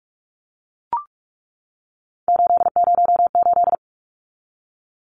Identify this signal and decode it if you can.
Morse code, heard as a keyed tone, about 44 wpm, 700 Hz. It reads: 808
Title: Numbers - 3-Digit 45wpm